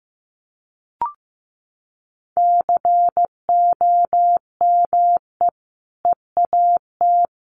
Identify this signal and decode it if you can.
Morse code, heard as a keyed tone, about 15 wpm, 700 Hz. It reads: COME EAT
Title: Sentences from Top 300 Words 15wpm